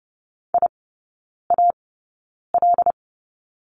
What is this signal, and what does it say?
Morse code, heard as a keyed tone, about 30 wpm, spaced as for 9 wpm, 700 Hz. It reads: IAL
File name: Lesson 19e - Letter w - Three Characters 30wpm